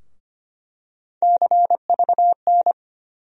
Morse code, heard as a keyed tone, about 25 wpm, 700 Hz. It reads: CVN
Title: USA Call Sign Postfixes - 3 Letters 25wpm